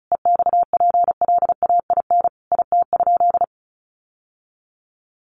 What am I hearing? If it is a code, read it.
Morse code, heard as a keyed tone, about 34 wpm, 700 Hz. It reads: EXPLAIN IT?
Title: Sentences from Top 1100 Words 35wpm